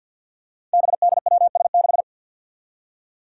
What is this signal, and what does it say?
Morse code, heard as a keyed tone, about 53 wpm, 700 Hz. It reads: BDWS6
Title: Five Challenging Letter-Number Combinations 50wpm